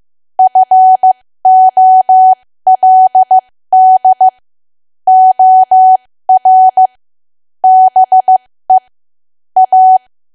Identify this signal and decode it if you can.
Morse code, heard as a keyed tone, about 15 wpm, spaced as for 13 wpm, 750 Hz. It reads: FOLD OR BE A
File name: cwa / W1AW-Level2 / 13WPM